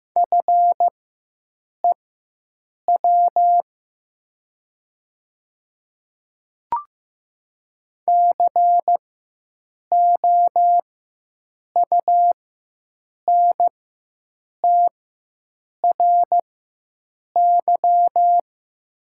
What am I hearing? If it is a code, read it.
Morse code, heard as a keyed tone, about 15 wpm, spaced as for 7 wpm, 700 Hz. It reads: FEW COUNTRY
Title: 4x Character Spacing - Top 300 Words - 15wpm